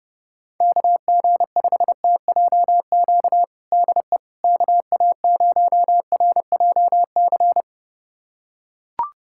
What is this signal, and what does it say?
Morse code, heard as a keyed tone, about 30 wpm, 700 Hz. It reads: KG5TJQ DE KA0RJC
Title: USA Callsign DE Callsign - 6 Characters - 30wpm